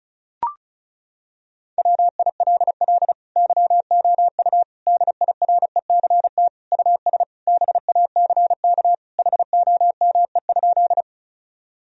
Morse code, heard as a keyed tone, about 36 wpm, 700 Hz. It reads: WILL YOU DIRECT US BACK HOME?
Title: Sentences from Top 1000 Words - Review Part 2 - 35wpm